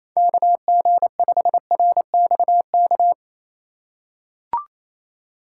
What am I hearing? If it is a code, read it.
Morse code, heard as a keyed tone, about 28 wpm, 700 Hz. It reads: KG5RXK
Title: USA Call Signs - 6 Characters - 4x Character Spacing - 28wpm